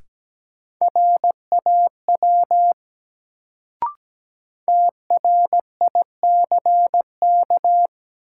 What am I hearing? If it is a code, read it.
Morse code, heard as a keyed tone, about 17 wpm, 700 Hz. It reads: RAW TRICK
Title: Sentences from Top 1500 Words - New Word Drill Alt - 17wpm